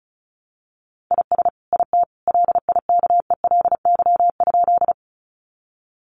Morse code, heard as a keyed tone, about 34 wpm, 700 Hz. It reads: IS IT LIKELY?
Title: Sentences from Top 600 Words 35wpm